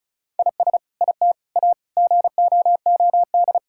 Morse code, heard as a keyed tone, about 36 wpm, 700 Hz. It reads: IS IT A GOOD
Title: Sentences from Top 1000 Words - Review Part 6 - 35wpm